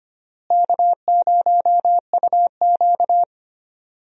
Morse code, heard as a keyed tone, about 25 wpm, 700 Hz. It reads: K0UQ
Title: USA Call Signs - 4 Characters - Repeated 3x - 25wpm